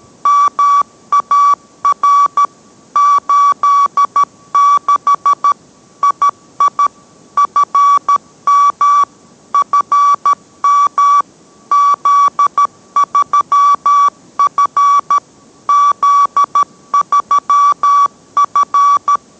Morse code, heard as a keyed tone, about 13 wpm, 1150 Hz. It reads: MAR 86 II FM FM Z3F Z3F